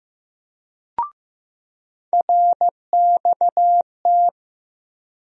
Morse code, heard as a keyed tone, about 15 wpm, 700 Hz. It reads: RXT
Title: USA Call Sign Postfixes - 3 Letters 15wpm